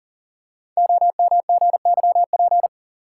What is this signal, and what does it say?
Morse code, heard as a keyed tone, about 42 wpm, 700 Hz. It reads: OMGYP
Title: Code Groups - 5 Letters - 40wpm